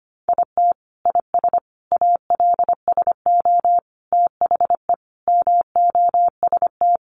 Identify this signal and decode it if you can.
Morse code, heard as a keyed tone, about 25 wpm, 700 Hz. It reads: IT IS ALSO THE MOST
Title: Sentences from Top 1400 Words 25wpm